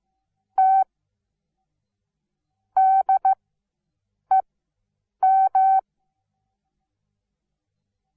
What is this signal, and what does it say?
Morse code, heard as a keyed tone, about 15 wpm, spaced as for 7 wpm, 770 Hz. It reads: T DEM
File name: Onlinelehrgang / morsen / DJ4UF Morsen17